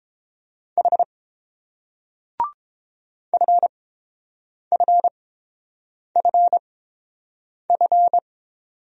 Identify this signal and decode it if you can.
Morse code, heard as a keyed tone, about 29 wpm, spaced as for 7 wpm, 690 Hz. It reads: H FFFF